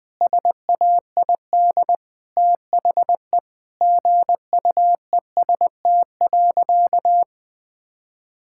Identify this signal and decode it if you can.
Morse code, heard as a keyed tone, about 20 wpm, 700 Hz. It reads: SAID THE GUEST.
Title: Sentences from Top 1600 Words 20wpm